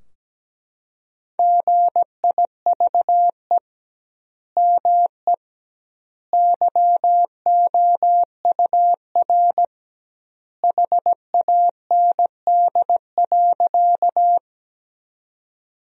Morse code, heard as a keyed tone, about 17 wpm, 700 Hz. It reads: GIVE ME YOUR HAND.
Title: 2x Word-Spacing - Sentences from Top 200 Words - 17wpm